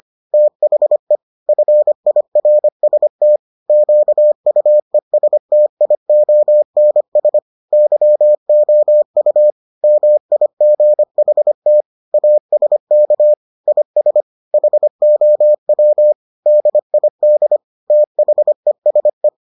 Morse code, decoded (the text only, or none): THE FIRST QUESTIONS YOU MIGHT ASK IS HOW DID THESE